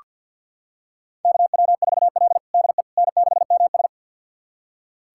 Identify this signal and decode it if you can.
Morse code, heard as a keyed tone, about 47 wpm, 700 Hz. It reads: KW4L DE N6MS
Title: USA Callsign DE Callsign - 4 Characters - 50wpm